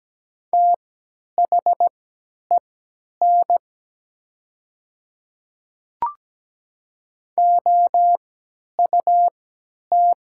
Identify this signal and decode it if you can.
Morse code, heard as a keyed tone, about 17 wpm, spaced as for 10 wpm, 700 Hz. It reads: THEN OUT
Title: 3x Character Spacing - Top 100 Words - 17wpm